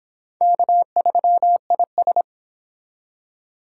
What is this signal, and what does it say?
Morse code, heard as a keyed tone, about 26 wpm, 700 Hz. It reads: K3IS